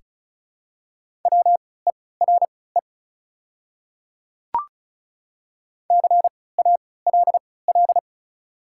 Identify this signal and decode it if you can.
Morse code, heard as a keyed tone, about 36 wpm, spaced as for 20 wpm, 700 Hz. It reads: WERE CALL
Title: 3x Character Spacing - Top 100 Words - 35wpm